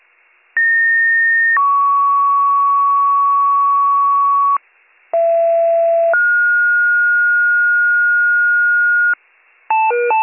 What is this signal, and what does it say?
Two-tone page, heard as a one-second tone, then a three-second tone, about 1850 then 1120 Hz, about 670 then 1430 Hz, with noise. It is heard from 0.6 to 9.1 s.